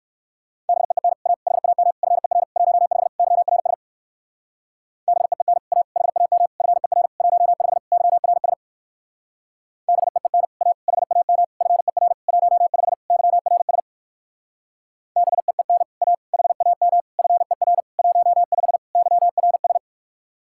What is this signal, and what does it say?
Morse code, heard as a keyed tone, about 56 wpm, 700 Hz. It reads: BEEN A HAM FER 15 YRS BEEN A HAM FER 15 YRS BEEN A HAM FER 15 YRS BEEN A HAM FER 15 YRS
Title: Speed-Racing Common QSO Phrases Encore - 45wpm